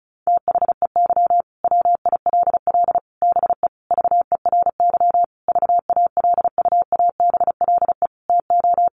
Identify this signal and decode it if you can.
Morse code, heard as a keyed tone, about 34 wpm, 700 Hz. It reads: THEY WILL BE VERY VALUABLE TO
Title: Sentences from Top 1100 Words 35wpm